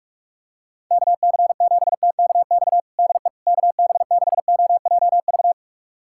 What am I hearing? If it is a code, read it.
Morse code, heard as a keyed tone, about 46 wpm, 700 Hz. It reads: KC7TKX DE KB6OJV
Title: USA Callsign DE Callsign - 6 Characters - 45wpm